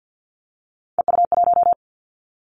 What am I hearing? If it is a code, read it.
Morse code, heard as a keyed tone, about 50 wpm, 700 Hz. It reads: EU1
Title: Callsign Prefixes for Top 50 DXCC - 50wpm